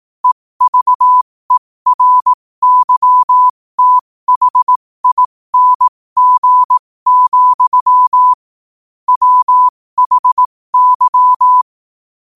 Morse code, heard as a keyed tone, about 18 wpm, spaced as for 15 wpm, 1000 Hz. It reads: EVERYTHING, WHY
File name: qotdfiles / 15wpm